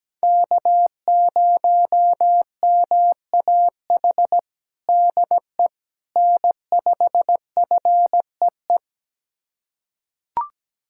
Morse code, heard as a keyed tone, about 17 wpm, 700 Hz. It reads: K0MAH DE N5FEE
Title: USA Callsign DE Callsign 17wpm